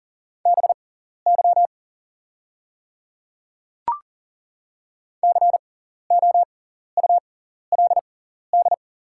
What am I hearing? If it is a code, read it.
Morse code, heard as a keyed tone, about 40 wpm, spaced as for 14 wpm, 700 Hz. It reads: BY COULD
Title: 6x Character Spacing - Top 100 Words - 40wpm